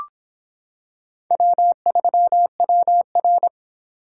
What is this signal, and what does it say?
Morse code, heard as a keyed tone, about 26 wpm, 700 Hz. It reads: W3WR